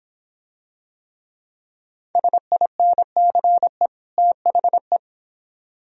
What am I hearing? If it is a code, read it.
Morse code, heard as a keyed tone, about 25 wpm, 700 Hz. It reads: SINCE THE